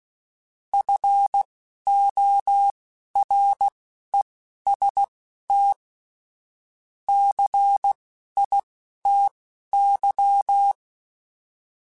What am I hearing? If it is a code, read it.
Morse code, heard as a keyed tone, about 16 wpm, spaced as for 11 wpm, 800 Hz. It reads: FOREST CITY